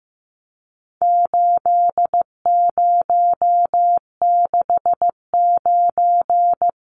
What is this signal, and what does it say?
Morse code, heard as a keyed tone, about 15 wpm, 700 Hz. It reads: 8069
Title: Numbers - 4-Digit 15wpm